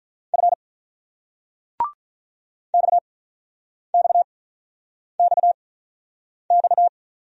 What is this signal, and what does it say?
Morse code, heard as a keyed tone, about 48 wpm, spaced as for 9 wpm, 700 Hz. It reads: F XXXX